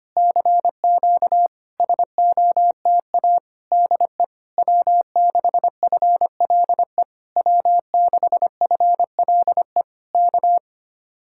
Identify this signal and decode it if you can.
Morse code, heard as a keyed tone, about 24 wpm, 700 Hz. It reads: CQ SOTA DE W6FLE W6FLE K
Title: Speed-Racing Common QSO Phrases Encore - 25wpm